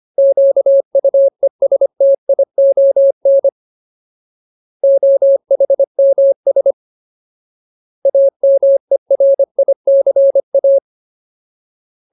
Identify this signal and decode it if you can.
Morse code, heard as a keyed tone, about 25 wpm, 550 Hz. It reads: QUESTION OHMS AMERICA
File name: Ultimate Word List / 25WPM Ultimate